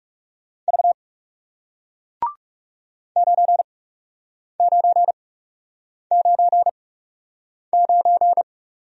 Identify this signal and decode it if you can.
Morse code, heard as a keyed tone, about 37 wpm, spaced as for 8 wpm, 700 Hz. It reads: V 9999
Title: Speed-Racing Single Letter-Number 30wpm